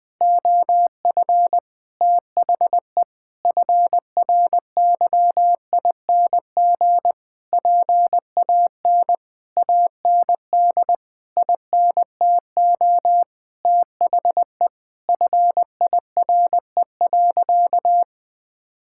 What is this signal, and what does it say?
Morse code, heard as a keyed tone, about 21 wpm, 700 Hz. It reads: OF THE FRYING PAN AND INTO THE FIRE.
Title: Sentences with 9-Word Phrases - 20wpm